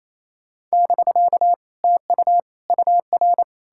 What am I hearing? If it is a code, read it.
Morse code, heard as a keyed tone, about 37 wpm, 700 Hz. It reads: <BK> TU UR